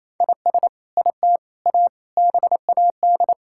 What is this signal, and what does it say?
Morse code, heard as a keyed tone, about 28 wpm, 700 Hz. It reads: IS IT A BAD
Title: Sentences from Top 1000 Words - Review Part 5 - 28wpm